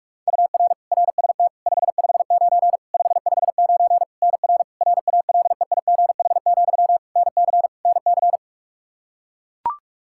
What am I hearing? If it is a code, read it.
Morse code, heard as a keyed tone, about 41 wpm, 700 Hz. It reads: UR RST 559 559 NR RALEIGH, NC NC
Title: K1USN SST QSO Practice 45wpm